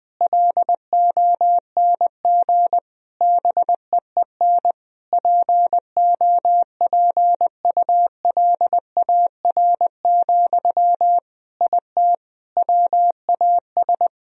Morse code, heard as a keyed tone, about 20 wpm, 700 Hz. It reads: LONG BEEN POPULAR, IT WAS
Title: Sentences from Top 1000 Words - Review Part 3 - 20wpm